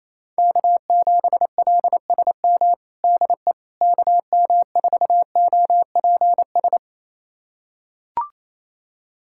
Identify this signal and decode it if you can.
Morse code, heard as a keyed tone, about 28 wpm, 700 Hz. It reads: K7LSM DE KM4OPS
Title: USA Callsign DE Callsign 28wpm